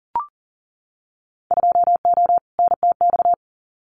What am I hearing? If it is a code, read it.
Morse code, heard as a keyed tone, about 41 wpm, 700 Hz. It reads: 2O NTX
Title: Winter Field Day Exchanges 40wpm